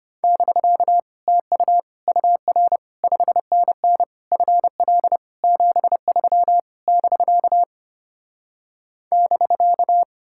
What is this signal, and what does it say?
Morse code, heard as a keyed tone, about 33 wpm, 700 Hz. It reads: <BK> TU UR 5NN FL 73 <BK> <BK>